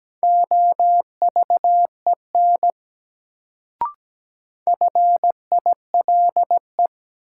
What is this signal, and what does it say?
Morse code, heard as a keyed tone, about 17 wpm, 700 Hz. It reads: OVEN FILE
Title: Sentences from Top 1200 Words - New Word Drill Alt - 17wpm